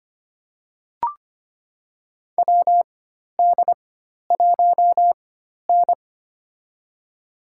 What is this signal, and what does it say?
Morse code, heard as a keyed tone, about 25 wpm, spaced as for 12 wpm, 700 Hz. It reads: WD1N
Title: USA Call Signs - 4 Characters - 4x Character Spacing - 25wpm